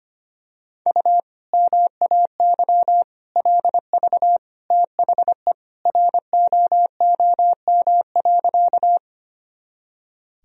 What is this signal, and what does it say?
Morse code, heard as a keyed tone, about 25 wpm, 700 Hz. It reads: U MAY LV THE ROOM.
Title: Sentences from Top 400 Words and Abbreviations 25wpm